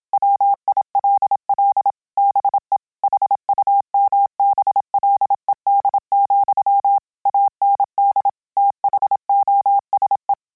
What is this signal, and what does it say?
Morse code, heard as a keyed tone, about 27 wpm, 800 Hz. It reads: WILL BE HUMBLED, AND THOSE